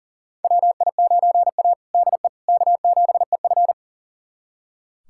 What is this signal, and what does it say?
Morse code, heard as a keyed tone, about 38 wpm, 700 Hz. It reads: WI9A DE K7EF
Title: USA Callsign DE Callsign 40wpm